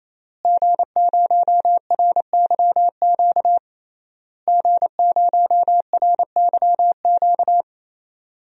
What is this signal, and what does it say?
Morse code, heard as a keyed tone, about 28 wpm, 700 Hz. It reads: G0RYQ G0RYQ
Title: DXCC-100 Call Signs - 5 Characters - Repeated 3x - 28wpm